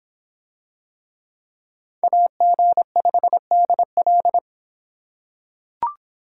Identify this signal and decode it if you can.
Morse code, heard as a keyed tone, about 22 wpm, 700 Hz. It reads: AG5DL